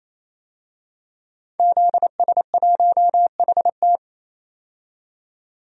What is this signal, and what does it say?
Morse code, heard as a keyed tone, about 28 wpm, 700 Hz. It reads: ZS1HT
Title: DXCC-100 Call Signs - 5 Characters - 28wpm